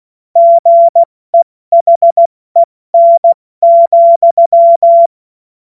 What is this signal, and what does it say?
Morse code, heard as a keyed tone, about 16 wpm, 680 Hz. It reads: GEHEN,
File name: Morsekurs / Witze / Witz 15. August 2023 70 F80